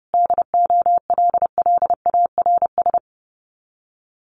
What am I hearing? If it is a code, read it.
Morse code, heard as a keyed tone, about 30 wpm, 700 Hz. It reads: DOLLARS
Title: Sets of 5 Words Encore 30wpm